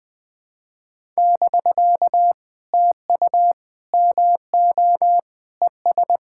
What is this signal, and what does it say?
Morse code, heard as a keyed tone, about 20 wpm, 700 Hz. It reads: <BK> TU MO ES
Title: POTA QSO Practice - 20wpm